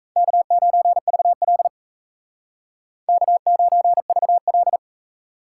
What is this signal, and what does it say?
Morse code, heard as a keyed tone, about 36 wpm, 700 Hz. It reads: K9VL K9VL